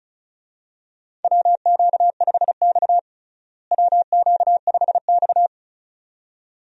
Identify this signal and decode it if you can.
Morse code, heard as a keyed tone, about 34 wpm, 700 Hz. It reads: WQ5X WQ5X
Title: USA Call Signs - 4 Characters - Repeated 2x - 35wpm